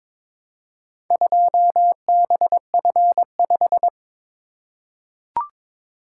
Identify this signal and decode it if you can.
Morse code, heard as a keyed tone, about 22 wpm, 700 Hz. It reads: 2BF5
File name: practice / 22wpm / Four Letter-Number Combinations - 22wpm